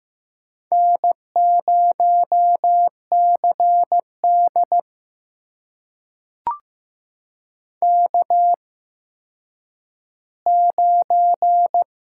Morse code, heard as a keyed tone, about 15 wpm, spaced as for 9 wpm, 700 Hz. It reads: N0CD K9
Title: USA Call Signs - 4 Characters - 8x Character Spacing - 15wpm